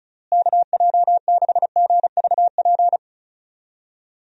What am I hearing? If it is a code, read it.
Morse code, heard as a keyed tone, about 36 wpm, 700 Hz. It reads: KJ6GVP
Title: USA Call Signs - 6 Characters - Repeated 3x - 35wpm